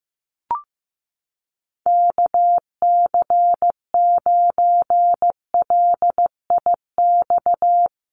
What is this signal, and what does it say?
Morse code, heard as a keyed tone, about 15 wpm, 700 Hz. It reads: KC9LIX
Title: USA Call Signs Repeated 3x 15wpm